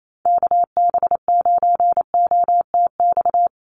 Morse code, heard as a keyed tone, about 28 wpm, 700 Hz. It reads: KB9OTX
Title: USA Call Signs Repeated 2x 28wpm